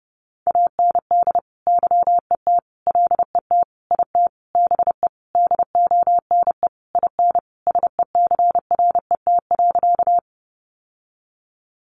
Morse code, heard as a keyed tone, about 30 wpm, 700 Hz. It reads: AND YET LET IT BE DONE IN SECRET.
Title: Sentences from Top 1200 Words 30wpm